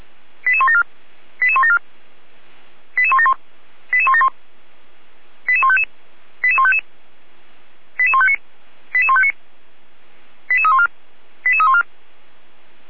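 Five-tone selective call (ZVEI), heard as quick runs of five tones, tones of about 70 ms. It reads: 88174 88174 88171 88171 88155 88155 88159 88159 88314 88314
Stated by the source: ZVEI five-tone selective call 88149, received on SDR by Feuerwehr Orlamünde fire brigade.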